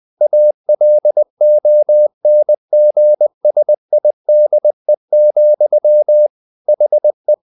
Morse code, heard as a keyed tone, about 20 wpm, 600 Hz. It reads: ALONGSIDE, HE
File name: the old man and the sea part 2 20wpm